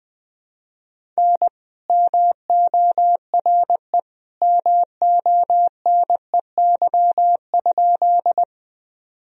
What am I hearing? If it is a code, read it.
Morse code, heard as a keyed tone, about 20 wpm, 700 Hz. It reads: N MORE MONEY?
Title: Sentences from Top 400 Words and Abbreviations 20wpm